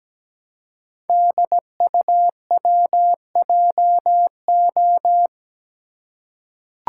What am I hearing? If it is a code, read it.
Morse code, heard as a keyed tone, about 17 wpm, 690 Hz. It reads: DUWJO
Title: Code Groups - 5 Letter:Numbers - 17wpm